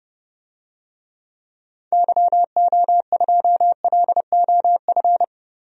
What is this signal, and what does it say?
Morse code, heard as a keyed tone, about 30 wpm, 700 Hz. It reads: YO2LOF
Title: DXCC-100 Call Signs - 6 Characters - 30wpm